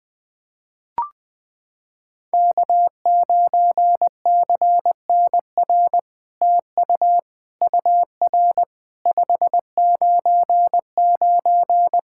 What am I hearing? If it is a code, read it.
Morse code, heard as a keyed tone, about 20 wpm, 700 Hz. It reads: K9CNR TU UR 599
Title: SOTA QSO Practice 20wpm